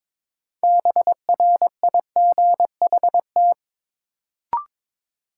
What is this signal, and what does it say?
Morse code, heard as a keyed tone, about 22 wpm, 700 Hz. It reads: BRIGHT
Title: Sentences from Top 1000 Words - New Word Drill - 22wpm